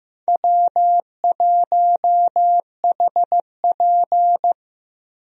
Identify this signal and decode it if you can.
Morse code, heard as a keyed tone, about 15 wpm, 700 Hz. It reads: W1HP
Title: USA Call Signs - 4 Characters - Repeated 3x - 15wpm